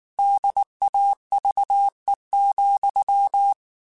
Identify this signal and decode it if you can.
Morse code, heard as a keyed tone, about 19 wpm, 800 Hz. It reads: DAVE,
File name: QSOs / QSOs-mp3s / WD0BGP 1@18